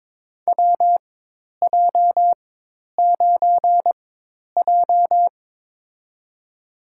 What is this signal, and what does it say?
Morse code, heard as a keyed tone, about 22 wpm, spaced as for 10 wpm, 700 Hz. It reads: WJ9J
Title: USA Call Signs - 4 Characters - 4x Character Spacing - 22wpm